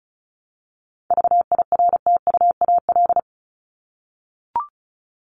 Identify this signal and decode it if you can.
Morse code, heard as a keyed tone, about 34 wpm, 700 Hz. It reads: VIRTUAL
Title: LY Suffix - Top 100 Words with Base Words 35wpm